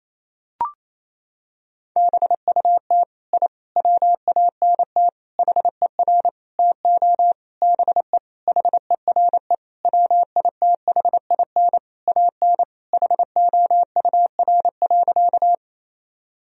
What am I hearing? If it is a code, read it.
Morse code, heard as a keyed tone, about 28 wpm, 700 Hz. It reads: BUT I WANT HER TO BE HERE WITHIN AN HOUR.
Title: Sentences from Top 1000 Words - Review Part 4 - 28wpm